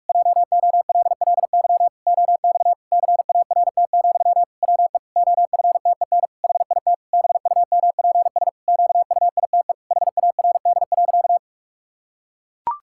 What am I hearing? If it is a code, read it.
Morse code, heard as a keyed tone, about 43 wpm, 700 Hz. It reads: JOLLY OX CART, WE OFTEN HIT BUMPS QUITE HARD.
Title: Pangram Challenge v2 - 45wpm